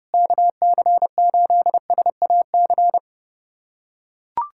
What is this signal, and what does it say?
Morse code, heard as a keyed tone, about 30 wpm, 700 Hz. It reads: KC8SAC
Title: USA Callsign DE Callsign 30wpm